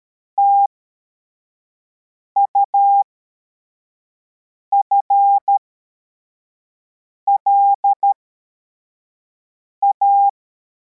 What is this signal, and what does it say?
Morse code, heard as a keyed tone, about 13 wpm, spaced as for 4 wpm, 800 Hz. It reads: TUFLA
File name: formation / cw / Text85